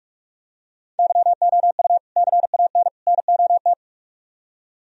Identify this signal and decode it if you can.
Morse code, heard as a keyed tone, about 44 wpm, 700 Hz. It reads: YOU CAN NOT